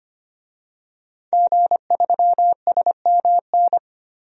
Morse code, heard as a keyed tone, about 25 wpm, 700 Hz. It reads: G3SMN
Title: DXCC-100 Call Signs - 5 Characters - Repeated 3x - 25wpm